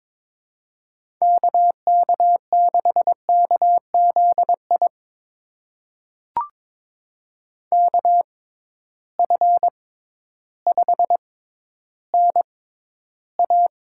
Morse code, heard as a keyed tone, about 22 wpm, spaced as for 12 wpm, 700 Hz. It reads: KK6KZI KF5NA
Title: USA Call Signs - 6 Characters - 6x Character Spacing - 22wpm